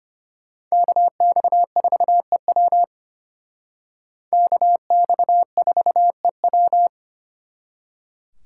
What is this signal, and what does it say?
Morse code, heard as a keyed tone, about 33 wpm, 700 Hz. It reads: KX4EW KX4EW